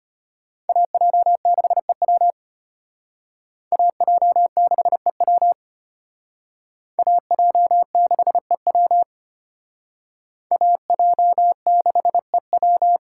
Morse code, heard as a keyed tone, about 34 wpm, 700 Hz. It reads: AJ6EW AJ6EW AJ6EW AJ6EW